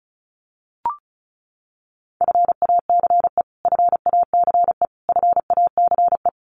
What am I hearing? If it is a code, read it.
Morse code, heard as a keyed tone, about 34 wpm, 700 Hz. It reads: FACE FACE FACE
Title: Top 500 Words Repeated 35wpm